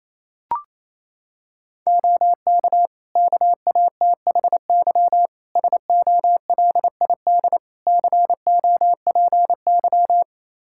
Morse code, heard as a keyed tone, about 28 wpm, 700 Hz. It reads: OK KATHY SOLID COPY
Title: Practice QSO - Exchange 3 - More Info - 28wpm